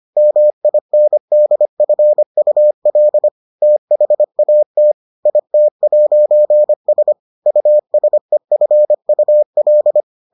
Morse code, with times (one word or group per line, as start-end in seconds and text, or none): 0.2-3.3 MINDFUL
3.6-4.9 THAT
5.3-7.1 IT'S
7.5-10.0 USEFUL